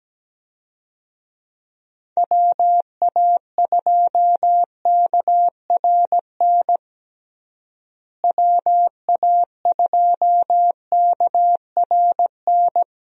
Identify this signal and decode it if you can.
Morse code, heard as a keyed tone, about 17 wpm, 700 Hz. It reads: WA2KRN WA2KRN
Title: USA Call Signs - 6 Characters - Repeated 2x - 17wpm